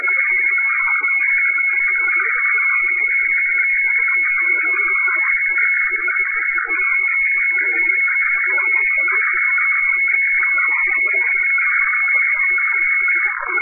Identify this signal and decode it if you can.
Weather fax (radiofax).
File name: hfwefax / 12788.1.khz. .2025